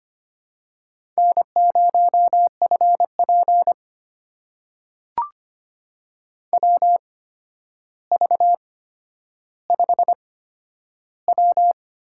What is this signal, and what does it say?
Morse code, heard as a keyed tone, about 25 wpm, spaced as for 11 wpm, 700 Hz. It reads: N0FP WV5W
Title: USA Call Signs - 4 Characters - 8x Character Spacing - 25wpm